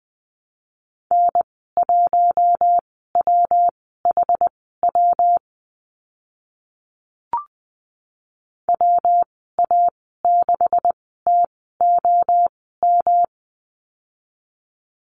Morse code, heard as a keyed tone, about 20 wpm, spaced as for 14 wpm, 700 Hz. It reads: N1WHW WA6TOM
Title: 128 Lesson 21g - Number 6 - Call Signs 20wpm